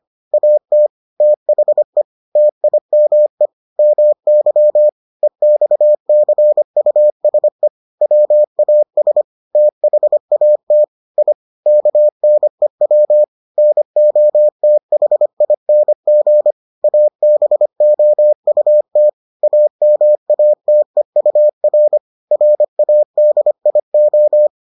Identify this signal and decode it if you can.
Morse code, heard as a keyed tone, about 25 wpm, 600 Hz. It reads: AT THE TIME MY EXCUSE WAS THAT I KNEW NOTHING ABOUT AMATEUR RADIO